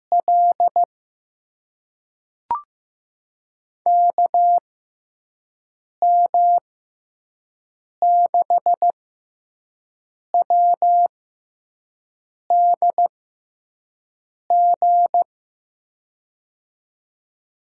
Morse code, heard as a keyed tone, about 15 wpm, spaced as for 5 wpm, 700 Hz. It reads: L KM6WDG
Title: USA Call Signs - 6 Characters - 6x Character Spacing - 15wpm